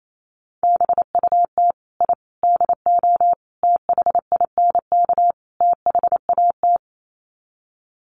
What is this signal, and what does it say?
Morse code, heard as a keyed tone, about 28 wpm, 700 Hz. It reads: BUT I DO THINK THAT
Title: Sets of 5 Words Encore2 28wpm